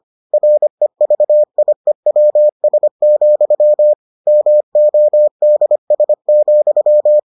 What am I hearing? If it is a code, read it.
Morse code, heard as a keyed tone, about 25 wpm, 600 Hz. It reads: REVIEWS, MODS,